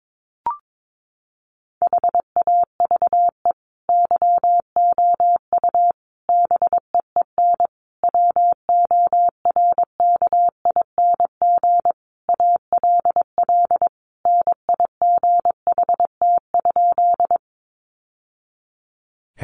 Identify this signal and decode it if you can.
Morse code, heard as a keyed tone, about 22 wpm, 700 Hz. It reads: HAVE YOU BEEN WORKING ALL NIGHT?
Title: Sentences from Top 700 Words 22wpm